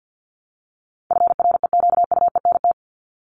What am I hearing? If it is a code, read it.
Morse code, heard as a keyed tone, about 50 wpm, 700 Hz. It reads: FREQUENT
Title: Top 1251-1500 Words 50wpm